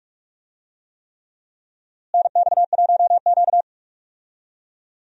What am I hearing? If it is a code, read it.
Morse code, heard as a keyed tone, about 46 wpm, 700 Hz. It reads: NX1Q
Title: USA Call Signs - 4 Characters - Repeated 3x - 45wpm